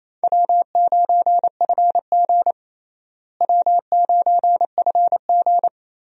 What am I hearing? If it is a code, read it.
Morse code, heard as a keyed tone, about 28 wpm, 700 Hz. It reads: W9FG W9FG
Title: USA Call Signs - 4 Characters - Repeated 2x - 28wpm